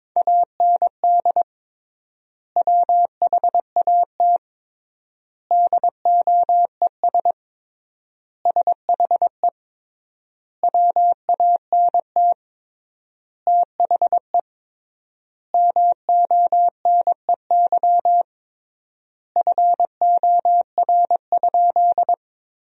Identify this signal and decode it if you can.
Morse code, heard as a keyed tone, about 22 wpm, 700 Hz. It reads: AND WHAT DOES SHE WANT THE MONEY FOR?
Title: 3x Word-Spacing - Sentences from Top 400 Words - 22wpm